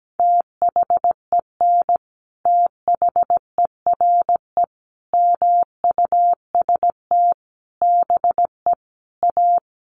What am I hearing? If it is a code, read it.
Morse code, heard as a keyed tone, about 17 wpm, 700 Hz. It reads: THEN THERE MUST BE A
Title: Sentences from Top 1400 Words 17wpm